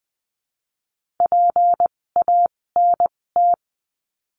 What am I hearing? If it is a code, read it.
Morse code, heard as a keyed tone, about 20 wpm, spaced as for 16 wpm, 700 Hz. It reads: PANT
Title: 256 Lesson 42a - Words 12-20wpm